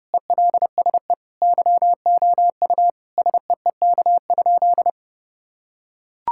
Morse code, heard as a keyed tone, about 30 wpm, 700 Hz. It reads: ELSE YOU SEEK?
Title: Sentences from Top 1600 Words 30wpm